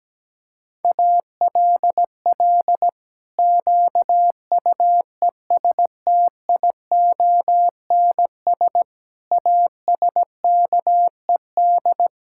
Morse code, heard as a keyed tone, about 17 wpm, 700 Hz. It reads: ALL QUESTIONS ASKED
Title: Pangram Challenge v2 - 17wpm